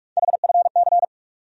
Morse code, heard as a keyed tone, about 43 wpm, 700 Hz. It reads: HFC
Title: USA Call Sign Postfixes - 3 Letters 45wpm